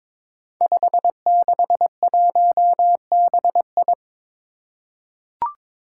Morse code, heard as a keyed tone, about 22 wpm, 700 Hz. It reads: 561BI
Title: Five Challenging Letter-Number Combinations 22wpm